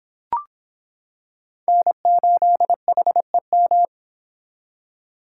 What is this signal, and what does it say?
Morse code, heard as a keyed tone, about 26 wpm, 700 Hz. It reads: N8HEM